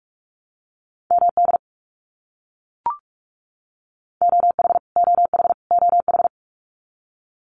Morse code, heard as a keyed tone, about 44 wpm, 700 Hz. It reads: MD OH OH OH
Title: US States Repeated 3x 45wpm